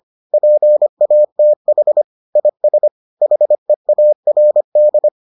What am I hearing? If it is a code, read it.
Morse code, heard as a keyed tone, about 25 wpm, 600 Hz. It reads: PATH IS HEARD